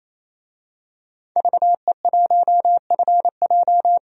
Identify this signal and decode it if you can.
Morse code, heard as a keyed tone, about 28 wpm, 700 Hz. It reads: VE1FJ
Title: DXCC-100 Call Signs - 5 Characters - 28wpm